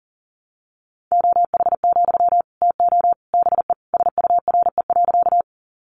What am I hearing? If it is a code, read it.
Morse code, heard as a keyed tone, about 40 wpm, 700 Hz. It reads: OH, TO BE SURE.
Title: Sentences from Top 500 Words 40wpm